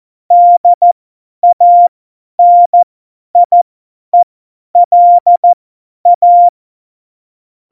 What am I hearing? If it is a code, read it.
Morse code, heard as a keyed tone, about 14 wpm, spaced as for 10 wpm, 700 Hz. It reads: DANIELA
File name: Morsekurs / Witze / Witz 24.Dezember 2019, 50 F70